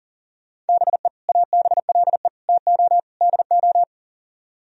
Morse code, heard as a keyed tone, about 50 wpm, 700 Hz. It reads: BE ABLE TO DO
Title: Speed-Racing - Sets of 4 Words 40wpm